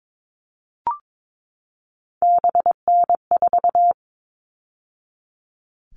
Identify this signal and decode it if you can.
Morse code, heard as a keyed tone, about 22 wpm, 700 Hz. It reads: BN4
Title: All Cut Numbers - 3-Digit 22wpm